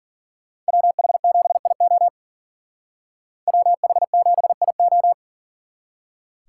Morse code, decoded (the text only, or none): WH7IO WH7IO